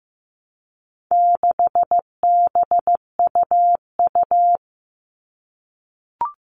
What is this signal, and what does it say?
Morse code, heard as a keyed tone, about 15 wpm, 700 Hz. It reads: 6BUU
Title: All Cut Numbers - 4-Digit 15wpm